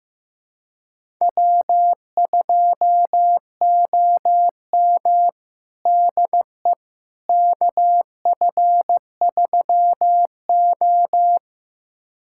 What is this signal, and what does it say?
Morse code, heard as a keyed tone, about 15 wpm, 700 Hz. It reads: W2OM DE KF3O
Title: USA Callsign DE Callsign - 4 Characters - 15wpm